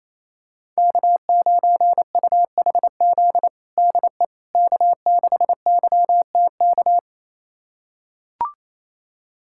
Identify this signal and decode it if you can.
Morse code, heard as a keyed tone, about 28 wpm, 700 Hz. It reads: K9UHZ DE K6YTK
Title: USA Callsign DE Callsign - 5 Characters - 28wpm